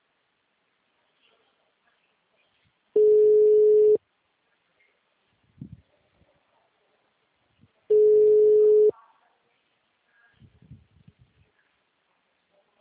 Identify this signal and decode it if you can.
Ringback tone (Europe, 425 Hz), with speech.
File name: out- -2012- - - .219